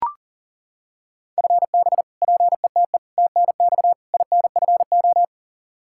Morse code, heard as a keyed tone, about 37 wpm, 700 Hz. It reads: FB PETE TNX INFO
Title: Practice QSO - Exchange 3 - More Info - 40wpm